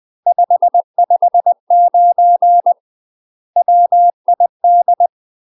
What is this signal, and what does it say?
Morse code, heard as a keyed tone, about 20 wpm, 700 Hz. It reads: 559 WID
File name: UK MASTER QSOs TRAINING / 20WPM / UK MASTER QSOs 0040